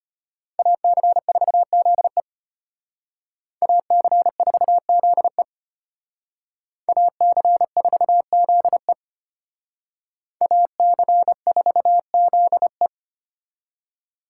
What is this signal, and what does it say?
Morse code, heard as a keyed tone, about 32 wpm, 700 Hz. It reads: AC4ZE AC4ZE AC4ZE AC4ZE